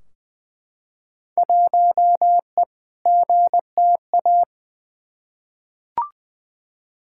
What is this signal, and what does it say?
Morse code, heard as a keyed tone, about 20 wpm, 700 Hz. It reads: 1E GTA
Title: Field Day Exchanges - 20wpm